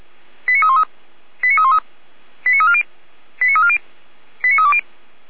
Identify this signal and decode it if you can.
Five-tone selective call (ZVEI), heard as quick runs of five tones, tones of about 70 ms. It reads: 89313 89313 89340 89340 89320
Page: ZVEI